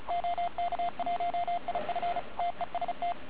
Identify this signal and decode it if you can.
Morse code, heard as a keyed tone, about 34 wpm, 2100 Hz, with noise. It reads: OK1IF TEST
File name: Archiv / CUC138 / OK1IF CUC138